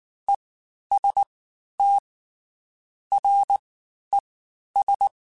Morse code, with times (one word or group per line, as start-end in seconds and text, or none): 0.3-0.3 E
0.9-1.2 S
1.8-2.0 T
3.1-3.6 R
4.1-4.2 E
4.8-5.1 S